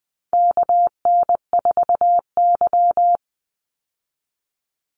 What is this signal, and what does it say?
Morse code, heard as a keyed tone, about 20 wpm, 700 Hz. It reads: KN4Y
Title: Call Signs Repeated 2x - 20wpm